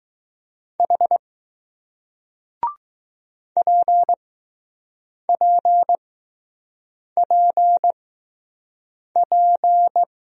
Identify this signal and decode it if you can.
Morse code, heard as a keyed tone, about 20 wpm, spaced as for 6 wpm, 700 Hz. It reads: H PPPP